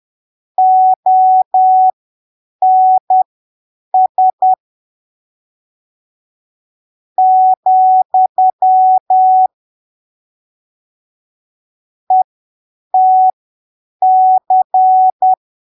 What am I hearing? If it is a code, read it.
Morse code, heard as a keyed tone, about 10 wpm, spaced as for 6 wpm, 750 Hz. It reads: ONS, ETC
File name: arquivos / texto / texto 18 10ppm